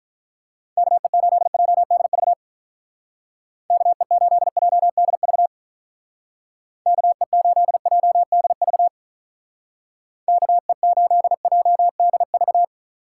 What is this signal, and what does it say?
Morse code, heard as a keyed tone, about 44 wpm, 700 Hz. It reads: KE8JDV KE8JDV KE8JDV KE8JDV